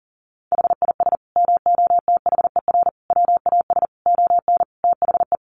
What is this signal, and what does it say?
Morse code, heard as a keyed tone, about 40 wpm, 700 Hz. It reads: HIS MOTHER WAS ON THE
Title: Sentences from Top 1200 Words 40wpm